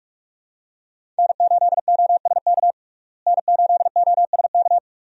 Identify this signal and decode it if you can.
Morse code, heard as a keyed tone, about 42 wpm, 700 Hz. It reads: N8OSK N8OSK
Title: USA Call Signs - 5 Characters - Repeated 2x - 45wpm